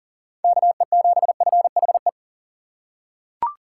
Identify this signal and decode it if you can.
Morse code, heard as a keyed tone, about 40 wpm, 700 Hz. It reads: KE7FHE
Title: USA Call Signs - 6 Characters - 6x Character Spacing - 40wpm